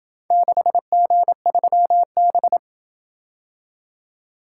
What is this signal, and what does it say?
Morse code, heard as a keyed tone, about 27 wpm, 700 Hz. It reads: 6G3B